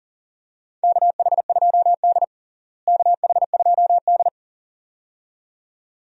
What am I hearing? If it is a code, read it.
Morse code, heard as a keyed tone, about 37 wpm, 700 Hz. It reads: KH2D KH2D
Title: USA Call Signs - 4 Characters - Repeated 2x - 40wpm